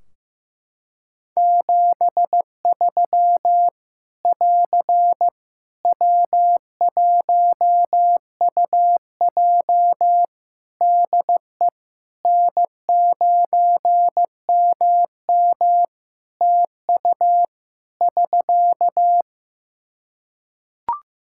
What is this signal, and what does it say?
Morse code, heard as a keyed tone, about 15 wpm, 700 Hz. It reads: 73 <AR> W1UJ DE N9MM TU <SK>
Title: Practice QSO - Exchange 4 - Ending - 15wpm